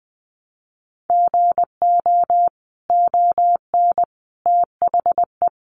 Morse code, heard as a keyed tone, about 20 wpm, 700 Hz. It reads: GO ON THE
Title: Sentences from Top 1200 Words 20wpm